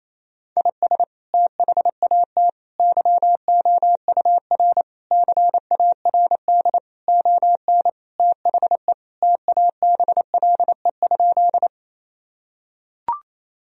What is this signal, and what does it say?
Morse code, heard as a keyed tone, about 28 wpm, 700 Hz. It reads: IS THAT YOUR CARD ON THE TABLE?
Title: Sentences from Top 1000 Words - Review Part 4 - 28wpm